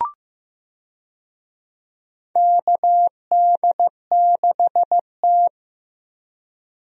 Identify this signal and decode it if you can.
Morse code, heard as a keyed tone, about 15 wpm, 700 Hz. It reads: KD6T
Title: USA Call Signs - 4 Characters - Repeated 2x - 15wpm